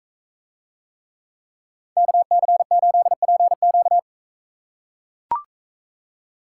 Morse code, heard as a keyed tone, about 42 wpm, 700 Hz. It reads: KC8PQ